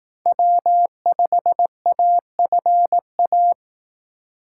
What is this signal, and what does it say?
Morse code, heard as a keyed tone, about 18 wpm, 700 Hz. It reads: W5AFA
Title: Speed Racing - USA 1x3 Call Signs 15wpm